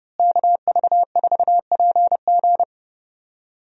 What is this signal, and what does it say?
Morse code, heard as a keyed tone, about 30 wpm, 700 Hz. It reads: KV4PG
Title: USA Call Signs - 5 Characters - Repeated 3x - 30wpm